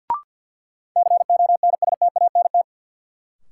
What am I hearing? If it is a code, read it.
Morse code, heard as a keyed tone, about 53 wpm, 700 Hz. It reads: CONSTANT
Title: Sentences from Top 1500 Words - New Word Drill Alt - 50wpm